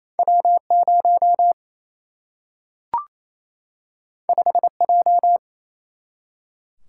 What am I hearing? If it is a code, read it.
Morse code, heard as a keyed tone, about 28 wpm, 700 Hz. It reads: W 0 5 J
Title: Single-Letter Single-Number 28wpm